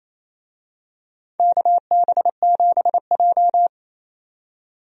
Morse code, heard as a keyed tone, about 27 wpm, 700 Hz. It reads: KB7J
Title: USA Call Signs - 4 Characters - 28wpm